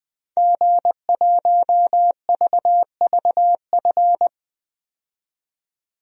Morse code, heard as a keyed tone, about 21 wpm, 700 Hz. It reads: G1VVF
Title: DXCC-100 Call Signs - 5 Characters - 20wpm